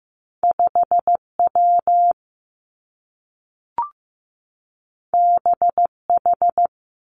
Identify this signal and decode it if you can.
Morse code, heard as a keyed tone, about 15 wpm, 700 Hz. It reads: 5 W B H
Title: Two Challenging Letter-Number Combinations 15wpm